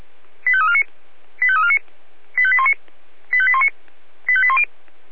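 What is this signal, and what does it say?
Five-tone selective call (ZVEI), heard as quick runs of five tones, tones of about 80 ms. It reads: 85349 85349 85819 85819 85810
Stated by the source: ZVEI five-tone selective call 85252 received on an SDR.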